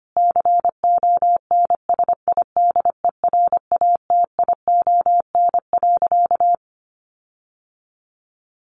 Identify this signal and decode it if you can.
Morse code, heard as a keyed tone, about 25 wpm, 700 Hz. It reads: CONSIDERATION.
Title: Sentences from Top 1500 Words 25wpm